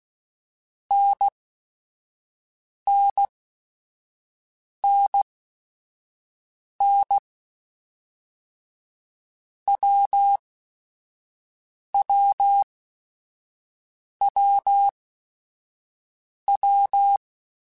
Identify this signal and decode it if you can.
Morse code, heard as a keyed tone, about 16 wpm, spaced as for 5 wpm, 800 Hz. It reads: NNNN WWWW
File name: Basic Sets / Basic Letter Groups Collective7 / AWUNDG@5